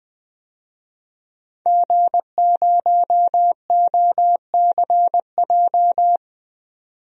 Morse code, heard as a keyed tone, about 20 wpm, 700 Hz. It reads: G0OCJ
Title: DXCC-100 Call Signs - 5 Characters - 20wpm